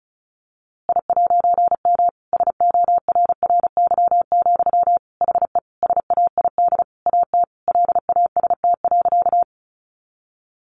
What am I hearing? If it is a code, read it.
Morse code, heard as a keyed tone, about 34 wpm, 700 Hz. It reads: I'M SORRY, HE SAID AT LAST.
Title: A Sound Of Thunder 35wpm